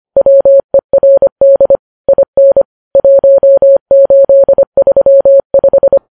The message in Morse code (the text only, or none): WERD IN 1835